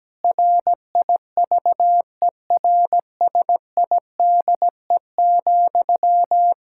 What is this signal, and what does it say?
Morse code, heard as a keyed tone, about 17 wpm, 700 Hz. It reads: RIVERSIDE,
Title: Practice QSO - Exchange 1 - Beginning - 17wpm